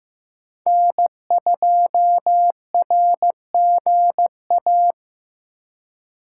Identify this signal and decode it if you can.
Morse code, heard as a keyed tone, about 16 wpm, 700 Hz. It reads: N2RGA
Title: USA Call Signs - 5 Characters - 4x Character Spacing - 15wpm